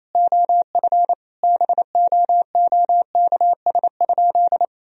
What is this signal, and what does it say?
Morse code, heard as a keyed tone, about 27 wpm, 700 Hz. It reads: OF BOOKS?
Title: Sentences from Top 1000 Words - Review Part 4 - 28wpm